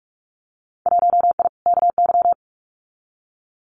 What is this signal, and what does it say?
Morse code, heard as a keyed tone, about 46 wpm, 700 Hz. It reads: 1I KY
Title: Winter Field Day Exchanges 45wpm